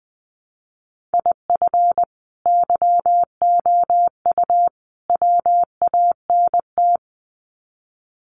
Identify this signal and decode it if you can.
Morse code, heard as a keyed tone, about 20 wpm, 700 Hz. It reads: IF YOU WANT
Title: Sets of 3 Words 20wpm